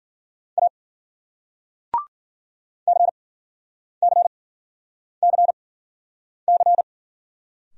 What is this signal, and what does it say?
Morse code, heard as a keyed tone, about 52 wpm, spaced as for 9 wpm, 700 Hz. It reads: A CCCC